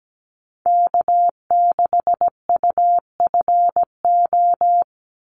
Morse code, heard as a keyed tone, about 17 wpm, 700 Hz. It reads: K6UFO
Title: Call Signs Repeated 3x - 17wpm